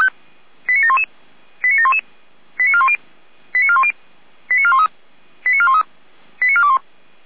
Five-tone selective call (ZVEI), heard as quick runs of five tones, tones of about 70 ms. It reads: 89711 89711 89410 89410 89413 89413 89421
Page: SDR/ZVEI